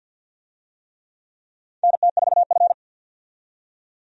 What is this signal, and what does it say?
Morse code, heard as a keyed tone, about 47 wpm, 700 Hz. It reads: NT4F
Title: USA Call Signs - 4 Characters - 50wpm